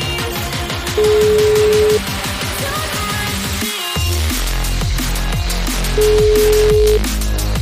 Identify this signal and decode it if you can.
Ringback tone (Europe, 425 Hz), with noise.